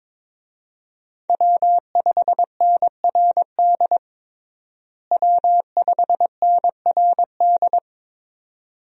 Morse code, heard as a keyed tone, about 22 wpm, 700 Hz. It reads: W5NRD W5NRD
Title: USA Call Signs - 5 Characters - Repeated 2x - 22wpm